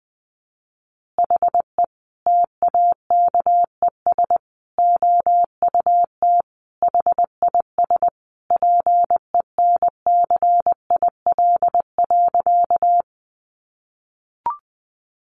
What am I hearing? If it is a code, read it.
Morse code, heard as a keyed tone, about 20 wpm, 700 Hz. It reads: HE TAKES OUT HIS PENCIL.
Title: Additional Easy Phrases 20wpm